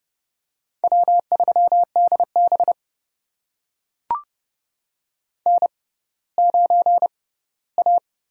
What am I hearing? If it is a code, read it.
Morse code, heard as a keyed tone, about 30 wpm, spaced as for 17 wpm, 700 Hz. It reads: W3DB N9A
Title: USA Call Signs - 6x Character Spacing - 30wpm